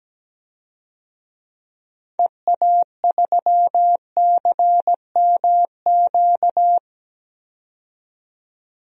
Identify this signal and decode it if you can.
Morse code, heard as a keyed tone, about 17 wpm, 700 Hz. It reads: EA3CMQ
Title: DXCC-100 Call Signs - 6 Characters - 17wpm